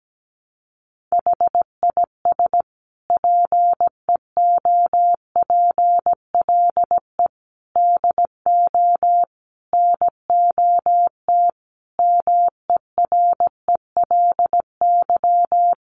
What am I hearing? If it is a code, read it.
Morse code, heard as a keyed tone, about 17 wpm, 700 Hz. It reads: HIS PEOPLE DO NOT MERELY
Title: Sentences from Top 1400 Words 17wpm